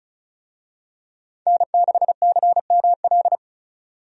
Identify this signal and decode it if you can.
Morse code, heard as a keyed tone, about 36 wpm, 700 Hz. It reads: N6CML
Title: USA Call Signs - 5 Characters - 35wpm